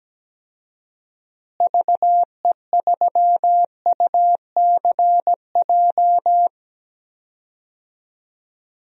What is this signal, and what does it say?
Morse code, heard as a keyed tone, about 17 wpm, 700 Hz. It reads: VE3UCJ
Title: DXCC-100 Call Signs - 6 Characters - Repeated 3x - 17wpm